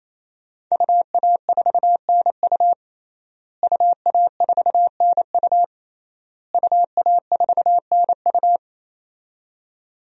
Morse code, heard as a keyed tone, about 28 wpm, 700 Hz. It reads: UA4NU UA4NU UA4NU
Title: DXCC-100 Call Signs - 5 Characters - Repeated 3x - 28wpm